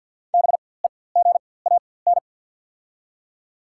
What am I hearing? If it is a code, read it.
Morse code, heard as a keyed tone, about 48 wpm, spaced as for 23 wpm, 700 Hz. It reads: BEGAN
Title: 4x Character Spacing - Top 300 Words - 50wpm